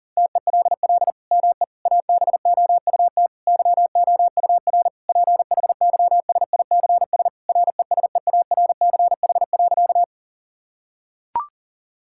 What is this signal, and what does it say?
Morse code, heard as a keyed tone, about 40 wpm, 700 Hz. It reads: TELL ME ABOUT YOUR PHYSICS RESEARCH.
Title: Sentences from Top 1000 Words - Review Part 5 - 40wpm